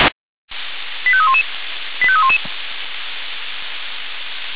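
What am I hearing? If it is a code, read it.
Five-tone selective call (ZVEI), heard as quick runs of five tones, tones of about 70 ms. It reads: 85311 85311
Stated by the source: ZVEI five-tone selective call 85311 received by SDR.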